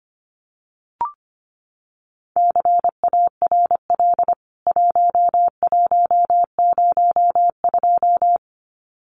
Morse code, heard as a keyed tone, about 25 wpm, 700 Hz. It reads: CARL 1102
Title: CWT Exchange 25wpm